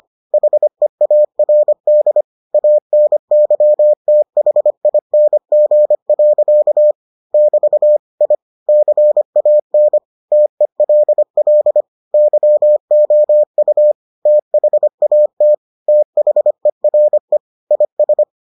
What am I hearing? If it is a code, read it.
Morse code, heard as a keyed tone, about 25 wpm, 600 Hz. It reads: HEARD ANYTHING. <BT> I CAN TELL YOU THAT THERE IS